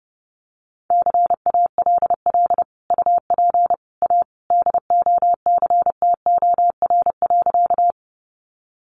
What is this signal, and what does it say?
Morse code, heard as a keyed tone, about 30 wpm, 700 Hz. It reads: CALL UP A DOCTOR.
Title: Sentences from Top 1500 Words 30wpm